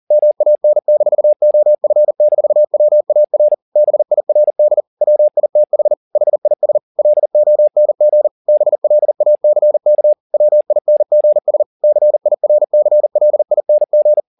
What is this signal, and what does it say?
Morse code, heard as a keyed tone, about 40 wpm, 600 Hz. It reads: MAN-OF-WAR BIRD WITH HIS LONG BLACK WINGS CIRCLING